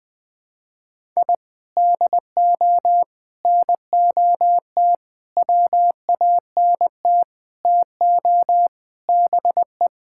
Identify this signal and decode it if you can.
Morse code, heard as a keyed tone, about 20 wpm, 700 Hz. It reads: I DO NOT WANT TO BE
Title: Sentences with 4-Word Phrases - Spoken Beginning - 20wpm